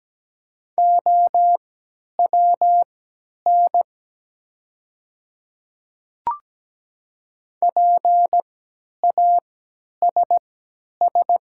Morse code, heard as a keyed tone, about 17 wpm, spaced as for 10 wpm, 700 Hz. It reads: OWN PASS
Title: 3x Character Spacing - Top 500 Words - 17wpm